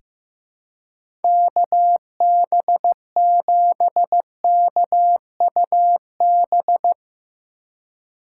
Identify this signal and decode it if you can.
Morse code, heard as a keyed tone, about 15 wpm, 700 Hz. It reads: KB7KUB
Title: USA Call Signs - 6 Characters - 4x Character Spacing - 15wpm